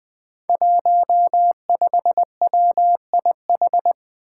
Morse code, heard as a keyed tone, about 20 wpm, 700 Hz. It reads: 15WIH
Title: Five Challenging Letter-Number Combinations 20wpm